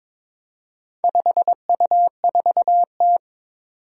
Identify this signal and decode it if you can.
Morse code, heard as a keyed tone, about 22 wpm, 700 Hz. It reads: 5U4T
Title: Four Challenging Letter-Number Combinations 22wpm